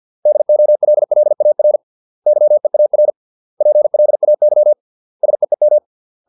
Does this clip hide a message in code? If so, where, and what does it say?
Yes: Morse code from 0.3 to 5.8 s, heard as a keyed tone, about 50 wpm, 600 Hz. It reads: DOLLAR YEAR PLAY SEEM